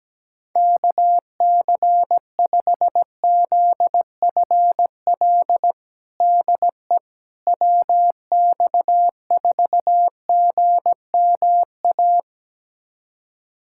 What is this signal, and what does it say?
Morse code, heard as a keyed tone, about 17 wpm, 700 Hz. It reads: KC5ZFL DE WX4GMA
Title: USA Callsign DE Callsign - 6 Characters - 17wpm